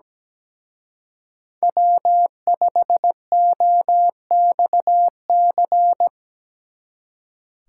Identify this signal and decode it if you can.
Morse code, heard as a keyed tone, about 20 wpm, 700 Hz. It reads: W5OXC
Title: Speed Racing - USA 1x3 Call Signs 17wpm